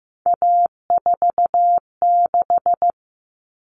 Morse code, heard as a keyed tone, about 15 wpm, 700 Hz. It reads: A46
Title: Common Cut Numbers - 3-Digit 15wpm